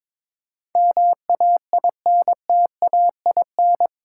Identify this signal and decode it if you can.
Morse code, heard as a keyed tone, about 22 wpm, 700 Hz. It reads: MAINTAIN
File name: practice / 22wpm / Eight Letter Words - 22wpm